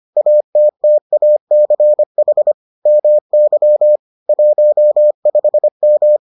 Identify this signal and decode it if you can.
Morse code, heard as a keyed tone, about 25 wpm, 600 Hz. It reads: ATTACH MY 15M